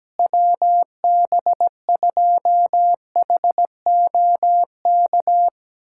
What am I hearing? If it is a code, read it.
Morse code, heard as a keyed tone, about 17 wpm, 700 Hz. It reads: WB2HOK
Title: USA Callsign DE Callsign - 6 Characters - 17wpm